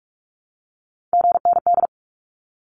Morse code, heard as a keyed tone, about 44 wpm, 700 Hz. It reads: GND
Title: Abbreviations 45wpm